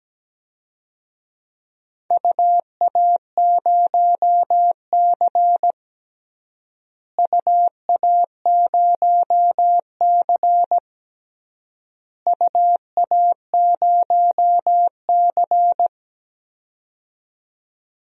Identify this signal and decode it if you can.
Morse code, heard as a keyed tone, about 17 wpm, 700 Hz. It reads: UA0C UA0C UA0C
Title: DXCC-100 Call Signs - 4 Characters - Repeated 3x - 17wpm